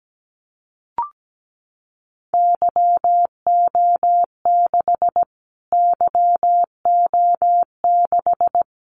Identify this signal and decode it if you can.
Morse code, heard as a keyed tone, about 17 wpm, 700 Hz. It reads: YO6 YO6
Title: Callsign Prefixes for Top 25 DXCC - Repeated 3x - 17wpm